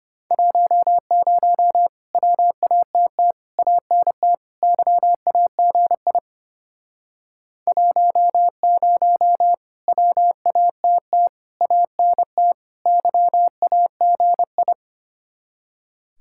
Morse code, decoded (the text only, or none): 10 WATT ANT YAGI 10 WATT ANT YAGI